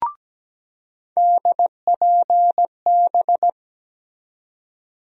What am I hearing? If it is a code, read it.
Morse code, heard as a keyed tone, about 17 wpm, 700 Hz. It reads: DPB
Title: USA Call Sign Postfixes - 3 Letters 17wpm